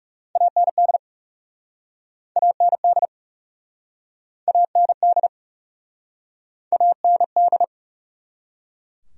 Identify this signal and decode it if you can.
Morse code, heard as a keyed tone, about 39 wpm, 700 Hz. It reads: AND AND AND AND